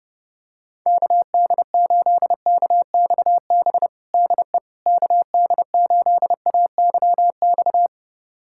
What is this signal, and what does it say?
Morse code, heard as a keyed tone, about 30 wpm, 700 Hz. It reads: KD8KXB DE KD8AYX
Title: USA Callsign DE Callsign 30wpm